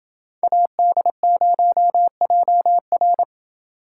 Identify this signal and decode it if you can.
Morse code, heard as a keyed tone, about 28 wpm, 700 Hz. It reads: AD0JR